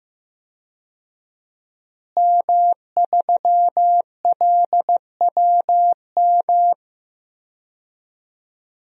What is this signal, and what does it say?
Morse code, heard as a keyed tone, about 15 wpm, 700 Hz. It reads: M3LWM
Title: DXCC-100 Call Signs - 5 Characters - Repeated 2x - 15wpm